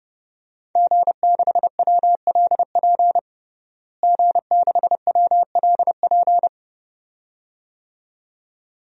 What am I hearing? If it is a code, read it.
Morse code, heard as a keyed tone, about 30 wpm, 700 Hz. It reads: G6WLP G6WLP
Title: DXCC-100 Call Signs - 5 Characters - Repeated 2x - 30wpm